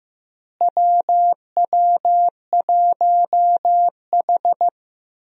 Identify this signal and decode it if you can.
Morse code, heard as a keyed tone, about 15 wpm, 700 Hz. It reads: WW1H
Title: USA Call Signs - 4 Characters - Repeated 3x - 15wpm